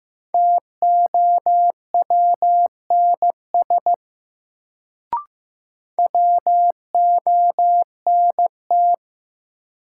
Morse code, heard as a keyed tone, about 15 wpm, 700 Hz. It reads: TOWNS WONT
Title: Sentences from Top 1000 Words - Review Part 6 - New Word Drill - 15wpm